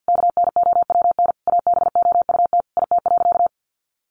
Morse code, heard as a keyed tone, about 50 wpm, 700 Hz. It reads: KNOWN ABOUT IT.
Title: Sentences from Top 1200 Words 50wpm